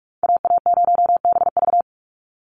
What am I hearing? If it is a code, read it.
Morse code, heard as a keyed tone, about 44 wpm, 700 Hz. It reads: AA0BV
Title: NA SOTA Activator Callsigns 45wpm